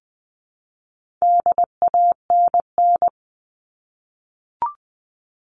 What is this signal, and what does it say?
Morse code, heard as a keyed tone, about 20 wpm, 700 Hz. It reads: DANN
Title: All Cut Numbers - 4-Digit 20wpm